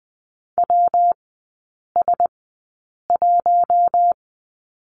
Morse code, heard as a keyed tone, about 20 wpm, spaced as for 8 wpm, 700 Hz. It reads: WS1
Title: 172 Lesson 28e - Number 9 - Three Characters 20wpm